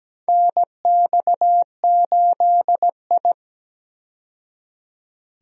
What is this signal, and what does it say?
Morse code, heard as a keyed tone, about 17 wpm, 700 Hz. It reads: NX8I
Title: USA Call Signs - 4 Characters - 17wpm